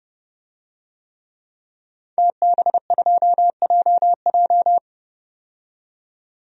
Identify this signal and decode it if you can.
Morse code, heard as a keyed tone, about 30 wpm, 700 Hz. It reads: TB2JJ
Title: DXCC-100 Call Signs - 5 Characters - 30wpm